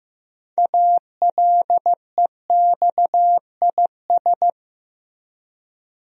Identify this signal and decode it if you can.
Morse code, heard as a keyed tone, about 15 wpm, 700 Hz. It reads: ALEXIS
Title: USA Common Names - 15wpm